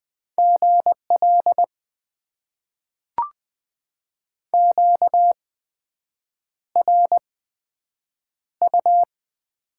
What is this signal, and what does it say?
Morse code, heard as a keyed tone, about 20 wpm, spaced as for 7 wpm, 700 Hz. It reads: GL QRU
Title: Top 90 QSO Elements - 8x Character Spacing - 20wpm